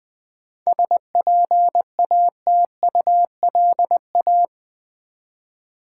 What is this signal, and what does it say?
Morse code, heard as a keyed tone, about 20 wpm, 700 Hz. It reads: SPATULA
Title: Easy Words with Hints - 20wpm